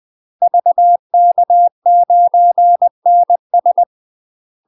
Morse code, heard as a keyed tone, about 20 wpm, 700 Hz. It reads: VK9NS